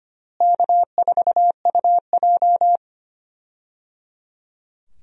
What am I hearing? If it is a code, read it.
Morse code, heard as a keyed tone, about 25 wpm, 700 Hz. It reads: K4UJ
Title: USA Call Signs - 4 Characters - Repeated 3x - 25wpm